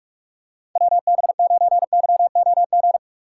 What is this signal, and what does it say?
Morse code, heard as a keyed tone, about 44 wpm, 700 Hz. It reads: WB9YOG
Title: USA Call Signs - 6 Characters - 4x Character Spacing - 45wpm